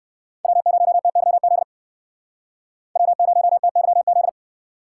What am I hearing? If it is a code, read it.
Morse code, heard as a keyed tone, about 61 wpm, 700 Hz. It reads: W0TYZ W0TYZ